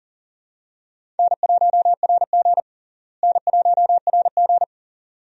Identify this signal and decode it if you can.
Morse code, heard as a keyed tone, about 40 wpm, 700 Hz. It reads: N1RG N1RG
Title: USA Call Signs - 4 Characters - Repeated 2x - 40wpm